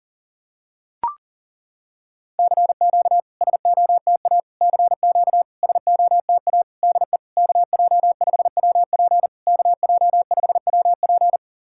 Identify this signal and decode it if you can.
Morse code, heard as a keyed tone, about 37 wpm, 690 Hz. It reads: CQ SOTA CQ SOTA DE KJ5WP KJ5WP
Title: SOTA QSO Practice 40wpm